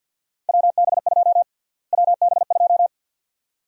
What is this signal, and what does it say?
Morse code, heard as a keyed tone, about 47 wpm, 700 Hz. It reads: WB2 WB2
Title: USA Call Sign Prefixes - Repeated 2x - 50wpm